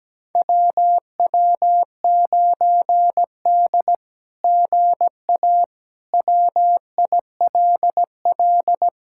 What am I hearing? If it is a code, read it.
Morse code, heard as a keyed tone, about 17 wpm, 700 Hz. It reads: WW9D GA WILL
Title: SOTA QSO Practice 17wpm